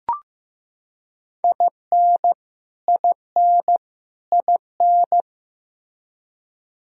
Morse code, heard as a keyed tone, about 15 wpm, 700 Hz. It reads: IN IN IN
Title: Top 100 Words Repeated 15wpm